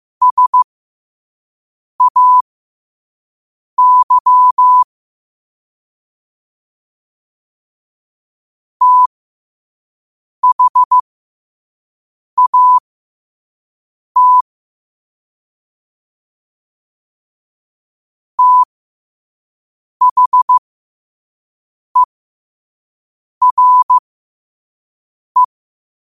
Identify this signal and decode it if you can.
Morse code, heard as a keyed tone, about 15 wpm, spaced as for 5 wpm, 1000 Hz. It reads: SAY THAT THERE